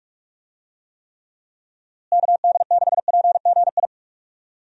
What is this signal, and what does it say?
Morse code, heard as a keyed tone, about 45 wpm, 700 Hz. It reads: KD6PGI
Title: USA Call Signs - 6 Characters - Repeated 3x - 45wpm